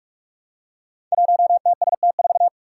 Morse code, heard as a keyed tone, about 44 wpm, 700 Hz. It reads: 1TST4
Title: Five Challenging Letter-Number Combinations 45wpm